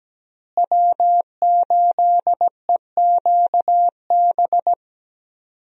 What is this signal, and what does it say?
Morse code, heard as a keyed tone, about 17 wpm, 700 Hz. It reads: W8EQB
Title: USA Call Signs - 5 Characters - Repeated 2x - 17wpm